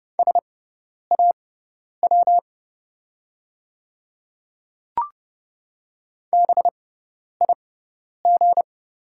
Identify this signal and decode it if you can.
Morse code, heard as a keyed tone, about 30 wpm, spaced as for 10 wpm, 700 Hz. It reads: SAW BIG
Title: 6x Character Spacing - Top 300 Words - 30wpm